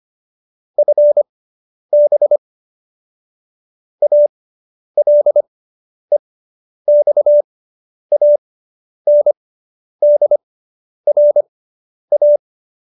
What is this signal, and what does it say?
Morse code, heard as a keyed tone, about 25 wpm, spaced as for 10 wpm, 590 Hz. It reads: FB ALEXANDRA